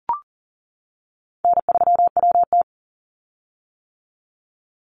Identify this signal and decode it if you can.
Morse code, heard as a keyed tone, about 40 wpm, 700 Hz. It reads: N3WT
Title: Easy Call Signs - 40wpm